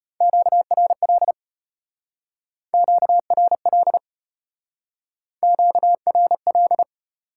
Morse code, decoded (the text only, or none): QRL QRL QRL